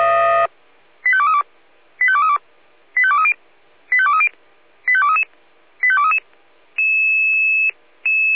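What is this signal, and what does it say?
Five-tone selective call (ZVEI), heard as quick runs of five tones, tones of about 70 ms. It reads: 85232 85232 85230 85230 85233 85233
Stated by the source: ZVEI five-tone selective call received on an SDR.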